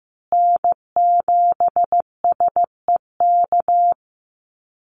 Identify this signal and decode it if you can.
Morse code, heard as a keyed tone, about 15 wpm, 700 Hz. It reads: N7SEK
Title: USA Call Signs Repeated 2x 15wpm